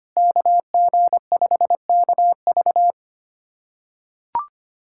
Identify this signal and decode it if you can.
Morse code, heard as a keyed tone, about 25 wpm, 700 Hz. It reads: KG5KV
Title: USA Callsign DE Callsign - 5 Characters - 25wpm